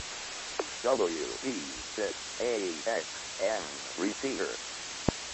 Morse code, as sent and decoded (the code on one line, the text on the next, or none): none